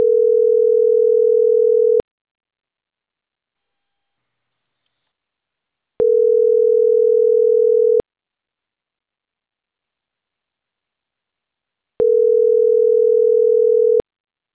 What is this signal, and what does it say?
Ringback tone (North America), 440 + 480 Hz.